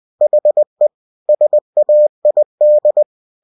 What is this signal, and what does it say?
Morse code, heard as a keyed tone, about 20 wpm, 600 Hz. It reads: HE SAID
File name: cw-books / winnie-the-pooh / winnie-20wpm-page-0007